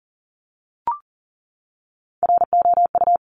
Morse code, heard as a keyed tone, about 38 wpm, 690 Hz. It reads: ROU
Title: Top Three Letter Combinations in Words 40wpm